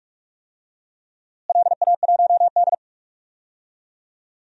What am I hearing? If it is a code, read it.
Morse code, heard as a keyed tone, about 46 wpm, 700 Hz. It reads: LA1D
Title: DXCC-100 Call Signs - 4 Characters - Repeated 3x - 45wpm